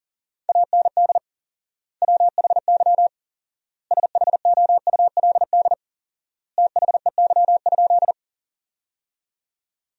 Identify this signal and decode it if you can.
Morse code, heard as a keyed tone, about 40 wpm, 700 Hz. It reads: AND WHY SHOULD THEY?
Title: 4x Word-Spacing - Sentences from Top 300 Words - 40wpm